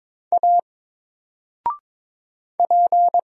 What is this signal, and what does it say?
Morse code, heard as a keyed tone, about 22 wpm, 700 Hz. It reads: A P
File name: Single-Letter-Number-22wpm